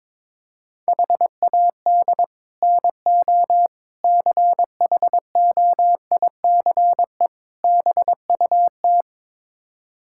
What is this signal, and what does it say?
Morse code, heard as a keyed tone, about 21 wpm, 700 Hz. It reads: HAD NO CHOICE BUT
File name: practice / 22wpm / Build A Phrase - 5-Word Phrases - 22wpm